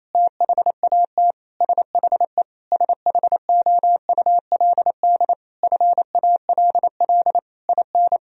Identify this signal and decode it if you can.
Morse code, heard as a keyed tone, about 28 wpm, 700 Hz. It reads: THAT SHE SHOULD FALL IN
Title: Sentences from Top 1000 Words - Review Part 4 - 28wpm